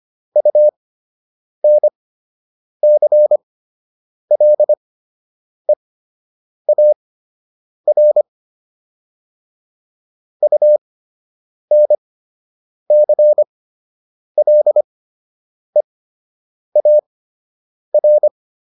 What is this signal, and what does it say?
Morse code, heard as a keyed tone, about 25 wpm, spaced as for 8 wpm, 600 Hz. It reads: UNCLEAR UNCLEAR